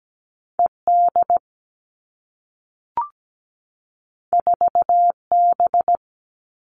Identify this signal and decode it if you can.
Morse code, heard as a keyed tone, about 17 wpm, 700 Hz. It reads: ED 4B
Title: All Cut Numbers - 2-Digit 17wpm